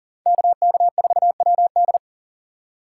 Morse code, heard as a keyed tone, about 40 wpm, 700 Hz. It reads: KK4WD
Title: USA Call Signs - 5 Characters - 4x Character Spacing - 40wpm